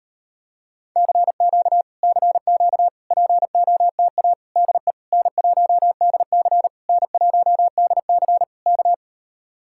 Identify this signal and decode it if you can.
Morse code, heard as a keyed tone, about 33 wpm, 700 Hz. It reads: CQ CQ POTA DE N1DC N1DC K